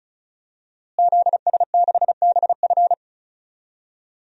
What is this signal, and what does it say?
Morse code, heard as a keyed tone, about 34 wpm, 700 Hz. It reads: ZS6BF
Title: DXCC-100 Call Signs - 5 Characters - Repeated 2x - 35wpm